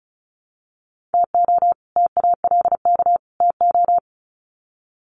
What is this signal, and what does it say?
Morse code, heard as a keyed tone, about 34 wpm, 700 Hz. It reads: TO TALK TO
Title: Sets of 3 Words 35wpm